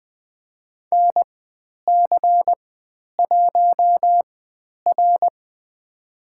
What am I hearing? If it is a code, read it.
Morse code, heard as a keyed tone, about 20 wpm, spaced as for 10 wpm, 700 Hz. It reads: NC1R
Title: 102 Lesson 17g - Letter u - Call Signs 20wpm